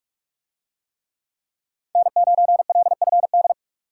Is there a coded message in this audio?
Yes: Morse code from 2.0 to 3.5 s, heard as a keyed tone, about 44 wpm, 700 Hz. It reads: N9LFD